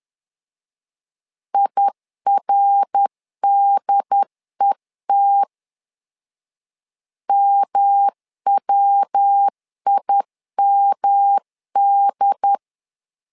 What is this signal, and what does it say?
Morse code, heard as a keyed tone, about 10 wpm, 800 Hz. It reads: IRDET MWIMD
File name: LEZIONI / 3 MWRO / 3 MWR 10wpm 1952